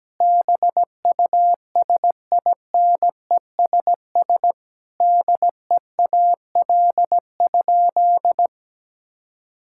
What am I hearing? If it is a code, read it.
Morse code, heard as a keyed tone, about 17 wpm, 700 Hz. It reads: BUSINESS DEAL?
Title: Sentences from Top 1000 Words - Review Part 6 - 17wpm